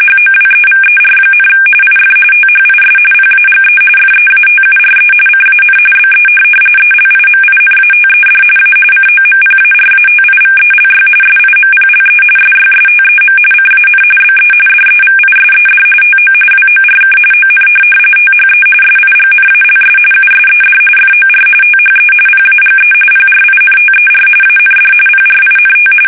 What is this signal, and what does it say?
RTTY, about 45.45 baud. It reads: HOWEVER, IF THIS PAGE FEATURE CAN BE ELIMINATED A COMPARATIVELY SIMPLE SYSTEM CAN BE DEVISED TO WORK ON THE SIGNALS USED FOR THE PAGE PRINTER AND PRINT
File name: fsk850